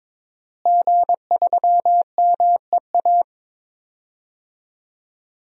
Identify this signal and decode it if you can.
Morse code, heard as a keyed tone, about 22 wpm, 700 Hz. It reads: G3MEA
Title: DXCC-100 Call Signs - 5 Characters - Repeated 2x - 22wpm